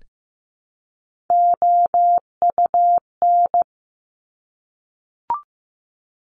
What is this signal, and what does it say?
Morse code, heard as a keyed tone, about 15 wpm, 700 Hz. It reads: OUN
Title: Top Three Letter Combinations in Words 15wpm